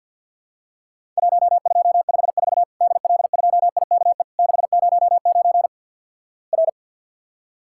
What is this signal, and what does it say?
Morse code, heard as a keyed tone, about 49 wpm, 700 Hz. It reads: 1254 DL2IKE 609 R
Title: Worked All Europe DX Contest - QTC Exchange - 50wpm